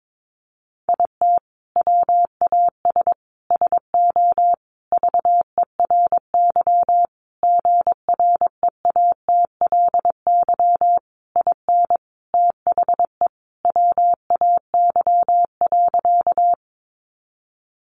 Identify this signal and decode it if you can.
Morse code, heard as a keyed tone, about 22 wpm, 700 Hz. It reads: IT WAS SO VERY GREATLY IN THE WAY.
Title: Sentences from Top 1400 Words 22wpm